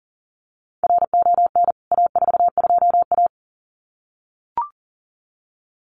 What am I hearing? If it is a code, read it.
Morse code, heard as a keyed tone, about 40 wpm, 700 Hz. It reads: RON A42A
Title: CWT Exchange 40wpm